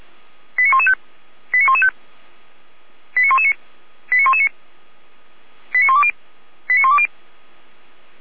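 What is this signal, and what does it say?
Five-tone selective call (ZVEI), heard as quick runs of five tones, tones of about 70 ms. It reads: 89115 89115 89119 89119 89120 89120
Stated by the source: Received on an SDR; ZVEI five-tone selective call 89099, fire brigade Orlamünde (feuerwehr-orlamuende).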